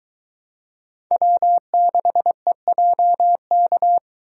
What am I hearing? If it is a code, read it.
Morse code, heard as a keyed tone, about 22 wpm, 700 Hz. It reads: W6EJK